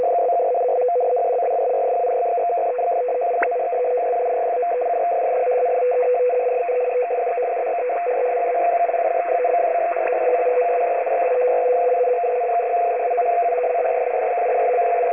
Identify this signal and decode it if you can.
Two-tone teleprinter signal, about 200 Hz shift.
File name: F03a- -1805utc-7840khz